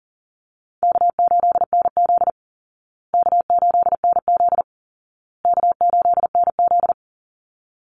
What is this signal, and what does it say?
Morse code, heard as a keyed tone, about 40 wpm, 700 Hz. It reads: K8NZ K8NZ K8NZ
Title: Easy Call Signs Repeated 3x - 40wpm